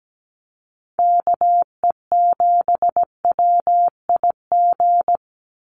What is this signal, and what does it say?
Morse code, heard as a keyed tone, about 17 wpm, 700 Hz. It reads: KE7WIG
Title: USA Call Signs 17wpm